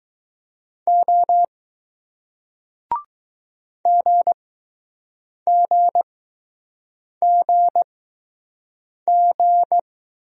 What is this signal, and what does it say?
Morse code, heard as a keyed tone, about 20 wpm, spaced as for 6 wpm, 700 Hz. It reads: O GGGG